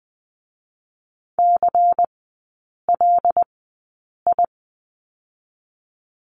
Morse code, heard as a keyed tone, about 20 wpm, spaced as for 8 wpm, 700 Hz. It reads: CLI
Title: 127 Lesson 21e - Number 6 - Three Characters 20wpm